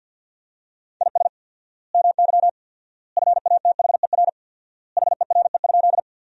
Morse code, heard as a keyed tone, about 53 wpm, 700 Hz. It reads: IS MY FATHER HERE?
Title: 4x Word-Spacing - Sentences from Top 300 Words - 50wpm